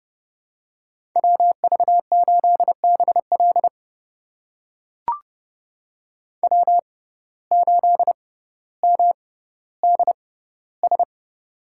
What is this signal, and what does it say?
Morse code, heard as a keyed tone, about 32 wpm, spaced as for 16 wpm, 700 Hz. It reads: WV8BL W8MDS
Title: USA Call Signs - 5 Characters - 6x Character Spacing - 30wpm